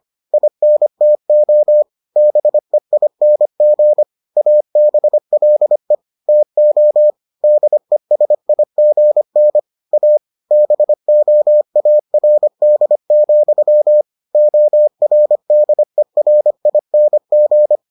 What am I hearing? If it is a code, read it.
Morse code, heard as a keyed tone, about 25 wpm, 600 Hz. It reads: INTO BEING ABLE TO DESIGN A BOARD, ORDERING